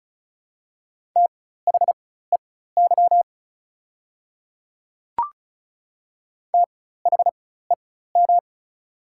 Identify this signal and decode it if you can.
Morse code, heard as a keyed tone, about 34 wpm, spaced as for 16 wpm, 700 Hz. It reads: THEY THEM
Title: 4x Character Spacing - Top 100 Words - 35wpm